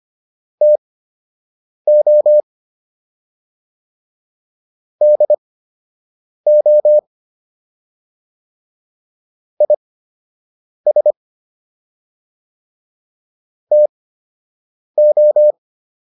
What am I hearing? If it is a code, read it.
Morse code, heard as a keyed tone, about 25 wpm, spaced as for 7 wpm, 600 Hz. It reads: TO DO IS TO